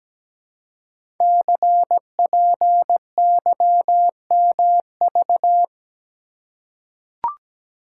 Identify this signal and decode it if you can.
Morse code, heard as a keyed tone, about 17 wpm, 700 Hz. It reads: CPYMV
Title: Code Groups - 5 Letter:Numbers - 17wpm